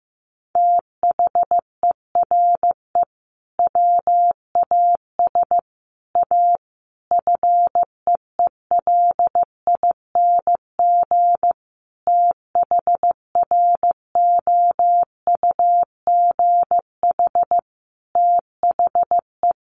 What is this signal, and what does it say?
Morse code, heard as a keyed tone, about 15 wpm, 700 Hz. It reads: THERE WAS A FEELING THROUGH THE
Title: Sentences from Top 1200 Words 15wpm